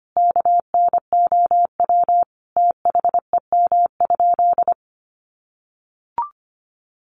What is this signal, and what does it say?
Morse code, heard as a keyed tone, about 25 wpm, 700 Hz. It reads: KNOW THEM?
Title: Sentences from Top 100 Words 25wpm